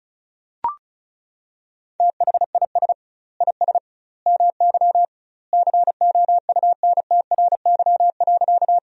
Morse code, heard as a keyed tone, about 34 wpm, 700 Hz. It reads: THIS IS MY COUNTRY.
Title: 2x Word-Spacing - Sentences from Top 300 Words - 35wpm